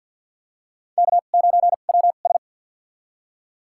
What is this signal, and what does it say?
Morse code, heard as a keyed tone, about 52 wpm, spaced as for 36 wpm, 700 Hz. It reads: K9WS
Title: USA Call Signs - 2x Character Spacing - 50wpm